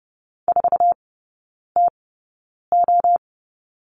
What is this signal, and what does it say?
Morse code, heard as a keyed tone, about 30 wpm, spaced as for 9 wpm, 700 Hz. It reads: 4TO
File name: 081 Lesson 14e - Number 2 - Three Characters 30wpm